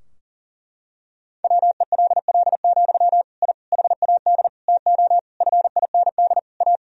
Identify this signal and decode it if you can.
Morse code, heard as a keyed tone, about 37 wpm, 700 Hz. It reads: WELL, I HAD TO FIND A
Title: Sentences from Top 1000 Words - Review Part 3 - 40wpm